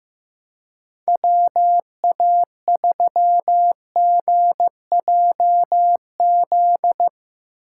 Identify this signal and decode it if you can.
Morse code, heard as a keyed tone, about 16 wpm, 700 Hz. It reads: WA3GJZ
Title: Speed Racing - USA Call Signs 15wpm